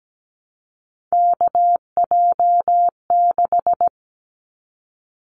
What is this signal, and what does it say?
Morse code, heard as a keyed tone, about 17 wpm, 700 Hz. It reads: KJ6
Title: Callsign Prefixes for Top 25 DXCC - Repeated 3x - 17wpm